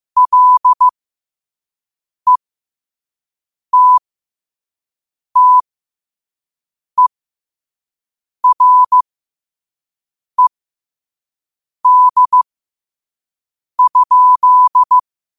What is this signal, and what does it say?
Morse code, heard as a keyed tone, about 15 wpm, spaced as for 5 wpm, 1000 Hz. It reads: LETTERED?
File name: 5wpm